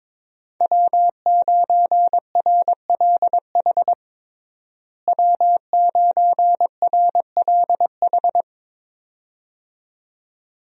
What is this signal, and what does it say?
Morse code, heard as a keyed tone, about 22 wpm, 700 Hz. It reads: W9RLH W9RLH
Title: USA Call Signs - 5 Characters - Repeated 2x - 22wpm